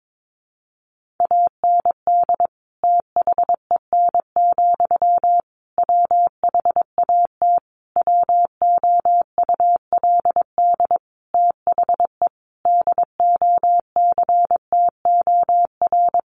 Morse code, heard as a keyed tone, about 22 wpm, 700 Hz. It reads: AND THEN, WHAT WOULD THE DOCTOR
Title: Sentences from Top 1500 Words 22wpm